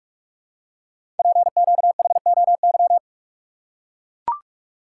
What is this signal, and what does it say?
Morse code, heard as a keyed tone, about 44 wpm, 700 Hz. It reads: PQHOY
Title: Code Groups - 5 Letters - 45wpm